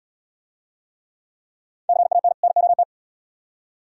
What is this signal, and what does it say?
Morse code, heard as a keyed tone, about 78 wpm, 700 Hz. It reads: CAN NOT